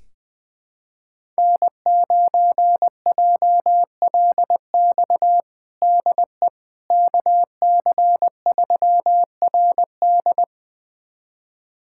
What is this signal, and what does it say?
Morse code, heard as a keyed tone, about 21 wpm, 700 Hz. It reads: N9JLX DE KC3RD
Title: USA Callsign DE Callsign 20wpm